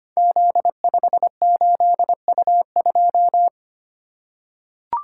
Morse code, heard as a keyed tone, about 25 wpm, 700 Hz. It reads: Z58U2
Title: Code Groups - 5 Letter:Numbers - 25wpm